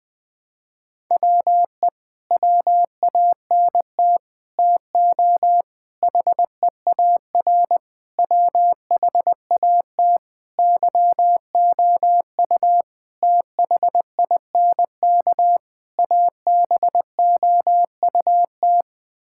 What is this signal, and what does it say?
Morse code, heard as a keyed tone, about 21 wpm, 700 Hz. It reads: WE WANT TO HEAR WHAT YOU THINK ABOUT
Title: Sets of 8 Words - 20wpm